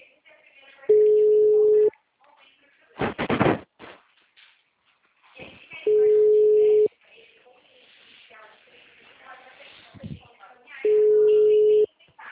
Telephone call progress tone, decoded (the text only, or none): ringback tone